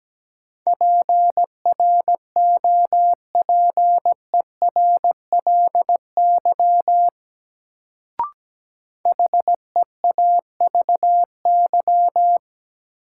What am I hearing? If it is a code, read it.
Morse code, heard as a keyed tone, about 17 wpm, 700 Hz. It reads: PROPERLY HEAVY
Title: Sentences from Top 1100 Words - New Word Drill Alt - 17wpm